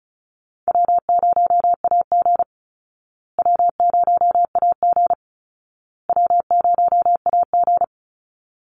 Morse code, heard as a keyed tone, about 34 wpm, 700 Hz. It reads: W0AG W0AG W0AG
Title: Easy Call Signs Repeated 3x - 35wpm